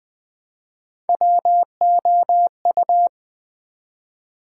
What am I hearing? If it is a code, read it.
Morse code, heard as a keyed tone, about 20 wpm, 700 Hz. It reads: WOU
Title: Build A Word - 1 Letter At A Time - 5-Letter Words - 20wpm